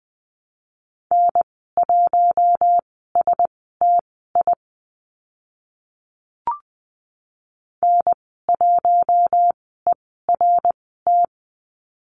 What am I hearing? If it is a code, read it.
Morse code, heard as a keyed tone, about 20 wpm, spaced as for 15 wpm, 700 Hz. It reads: N1STI N1ERT
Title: 055 Lesson 10g - Letter r - Call Signs 20wpm